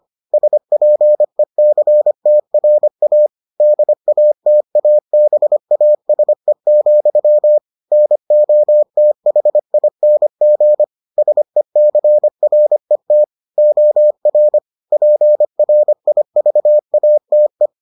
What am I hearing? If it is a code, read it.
Morse code, heard as a keyed tone, about 25 wpm, 600 Hz. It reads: SPECTRA DATABASE, NOTHING SECRET OR PRIVATE